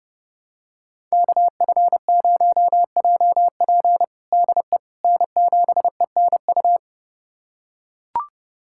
Morse code, heard as a keyed tone, about 32 wpm, 700 Hz. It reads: KF0JP DE N7ENU
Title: USA Callsign DE Callsign - 5 Characters - 30wpm